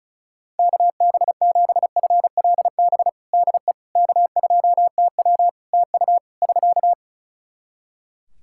Morse code, heard as a keyed tone, about 36 wpm, 700 Hz. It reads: KB7FLB DE K2TW TU <SK>
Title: Practice QSO - Exchange 4 - Ending - 35wpm